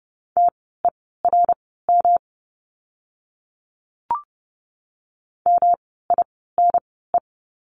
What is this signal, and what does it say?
Morse code, heard as a keyed tone, about 29 wpm, spaced as for 17 wpm, 700 Hz. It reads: TERM MINE
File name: course / 30:10wpm / 108 Lesson 18f - Letter m - Words 30wpm